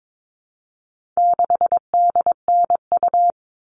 Morse code, heard as a keyed tone, about 22 wpm, 700 Hz. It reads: 6DNU
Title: All Cut Numbers - 4-Digit 22wpm